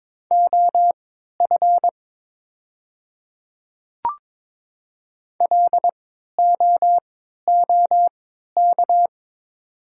Morse code, heard as a keyed tone, about 22 wpm, spaced as for 13 wpm, 700 Hz. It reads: OF LOOK
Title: 3x Character Spacing - Top 200 Words - 22wpm